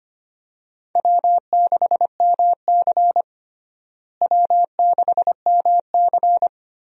Morse code, heard as a keyed tone, about 25 wpm, 700 Hz. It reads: W6MC W6MC
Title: USA Call Signs - 4 Characters - Repeated 2x - 25wpm